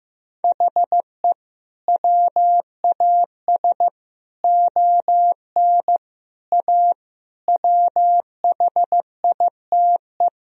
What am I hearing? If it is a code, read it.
Morse code, heard as a keyed tone, about 15 wpm, 700 Hz. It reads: HE WAS ON A WHITE
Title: Sentences from Top 1000 Words - Review Part 1 - 15wpm